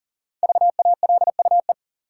Morse code, heard as a keyed tone, about 40 wpm, 700 Hz. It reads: VALUE
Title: Sentences from Top 600 Words - New Word Drill - 40wpm